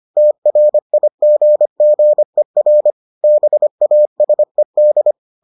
Morse code, heard as a keyed tone, about 25 wpm, 600 Hz. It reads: TRIGGER BASED